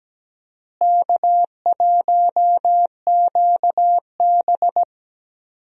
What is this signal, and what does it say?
Morse code, heard as a keyed tone, about 17 wpm, 700 Hz. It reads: K1QB
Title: USA Call Signs - 4 Characters - Repeated 2x - 17wpm